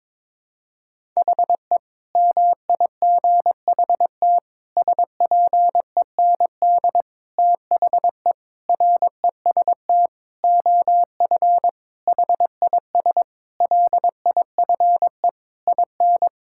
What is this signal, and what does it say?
Morse code, heard as a keyed tone, about 22 wpm, 700 Hz. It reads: HE MIGHT SPEND THE REST OF HIS LIFE IN
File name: practice / 22wpm / Sentences with 8-Word Phrases - 22wpm